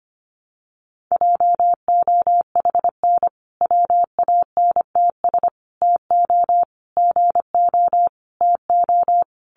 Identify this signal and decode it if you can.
Morse code, heard as a keyed tone, about 25 wpm, 700 Hz. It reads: JOHN WANTS TO GO TO
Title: Additional Easy Phrases 25wpm